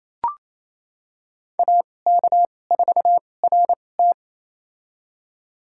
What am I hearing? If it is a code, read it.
Morse code, heard as a keyed tone, about 28 wpm, spaced as for 20 wpm, 700 Hz. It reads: AK4RT
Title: USA Call Signs - 5 Characters - 2x Character Spacing - 28wpm